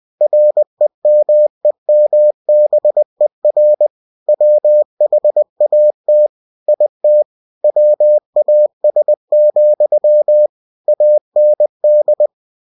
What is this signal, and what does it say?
Morse code, heard as a keyed tone, about 20 wpm, 600 Hz. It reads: REMEMBER WHAT IT WAS, AND
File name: cw-books / winnie-the-pooh / winnie-20wpm-page-0078